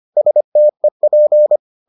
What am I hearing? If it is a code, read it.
Morse code, heard as a keyed tone, about 25 wpm, 600 Hz. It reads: STEP